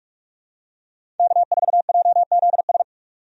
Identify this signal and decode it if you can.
Morse code, heard as a keyed tone, about 46 wpm, 700 Hz. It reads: K4JZS
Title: USA Call Signs - 5 Characters - 45wpm